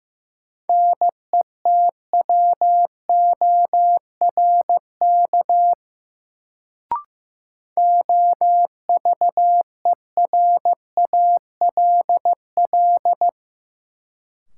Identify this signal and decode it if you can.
Morse code, heard as a keyed tone, about 15 wpm, 700 Hz. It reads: NETWORK OVERALL
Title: Sentences from Top 1000 Words - Review Part 7 - New Word Drill - 15wpm